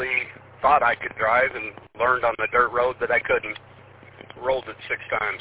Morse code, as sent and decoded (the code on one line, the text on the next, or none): none